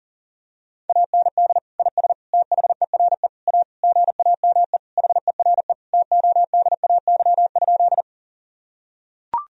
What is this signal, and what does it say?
Morse code, heard as a keyed tone, about 37 wpm, 700 Hz. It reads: AND IS THERE A GAME HERE TODAY?
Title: Sentences from Top 1000 Words - Review Part 3 - 40wpm